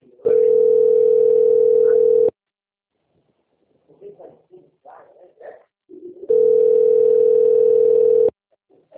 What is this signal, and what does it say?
Ringback tone (North America), 440 + 480 Hz, with speech.
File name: out- - - - - .9688